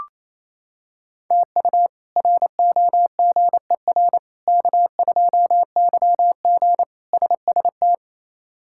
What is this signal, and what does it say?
Morse code, heard as a keyed tone, about 28 wpm, 700 Hz. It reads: TU ROGER K2YG SST
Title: K1USN SST QSO Practice - 28wpm